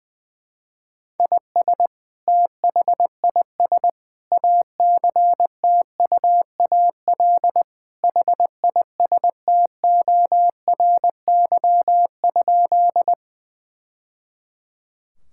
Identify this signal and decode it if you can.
Morse code, heard as a keyed tone, about 20 wpm, 700 Hz. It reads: IS THIS ACTUAL HISTORY?
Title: Sentences from Top 1000 Words - Review Part 5 - 20wpm